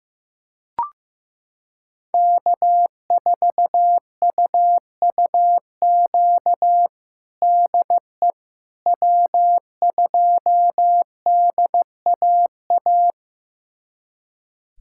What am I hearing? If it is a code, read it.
Morse code, heard as a keyed tone, about 15 wpm, 700 Hz. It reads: K4UUQ DE W2DAA
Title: USA Callsign DE Callsign - 5 Characters - 15wpm